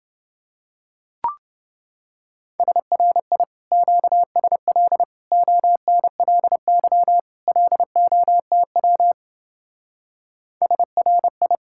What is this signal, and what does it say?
Morse code, heard as a keyed tone, about 29 wpm, 700 Hz. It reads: SRI QSL ONLY LOTW SRI